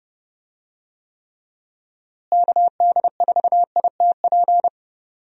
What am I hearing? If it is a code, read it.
Morse code, heard as a keyed tone, about 32 wpm, 700 Hz. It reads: KD4ITP
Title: USA Call Signs - 6 Characters - Repeated 2x - 30wpm